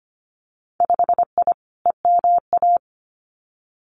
Morse code, heard as a keyed tone, about 25 wpm, 700 Hz. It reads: 5I EMA
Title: Winter Field Day Exchanges Encore 25wpm